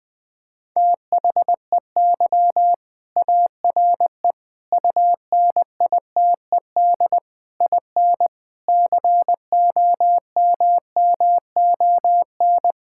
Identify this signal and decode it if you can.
Morse code, heard as a keyed tone, about 20 wpm, 700 Hz. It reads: THEY ARE UNITED IN COMMON
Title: Sentences from Top 1000 Words - Review Part 5 - 20wpm